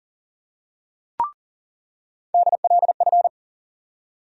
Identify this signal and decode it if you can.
Morse code, heard as a keyed tone, about 40 wpm, 700 Hz. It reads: DLF
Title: USA Call Sign Postfixes - 3 Letters 40wpm